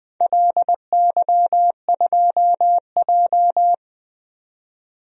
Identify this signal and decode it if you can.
Morse code, heard as a keyed tone, about 21 wpm, 700 Hz. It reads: LY2J
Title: DXCC-100 Call Signs - 4 Characters - 20wpm